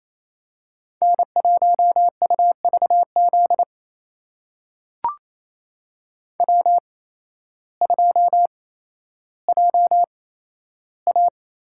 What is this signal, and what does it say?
Morse code, heard as a keyed tone, about 28 wpm, spaced as for 13 wpm, 700 Hz. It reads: N1UVZ W2JA
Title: USA Call Signs - 5 Characters - 8x Character Spacing - 28wpm